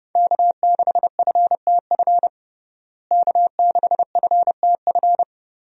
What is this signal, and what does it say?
Morse code, heard as a keyed tone, about 30 wpm, 700 Hz. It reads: K6FTF K6FTF
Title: USA Call Signs - 5 Characters - Repeated 3x - 30wpm